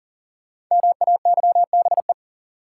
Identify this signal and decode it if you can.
Morse code, heard as a keyed tone, about 40 wpm, 700 Hz. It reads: MAYBE
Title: Sentences from Top 1000 Words - New Word Drill - 40wpm